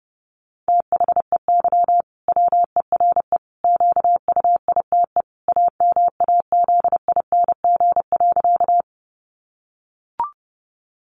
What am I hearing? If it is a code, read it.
Morse code, heard as a keyed tone, about 30 wpm, 700 Hz. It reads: THEY WERE QUITE AMAZING.
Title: Sentences from Top 1500 Words 30wpm